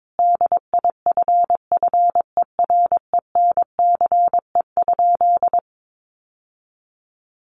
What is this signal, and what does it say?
Morse code, heard as a keyed tone, about 22 wpm, 700 Hz. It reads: DIFFERENCE?
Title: Sentences from Top 800 Words 22wpm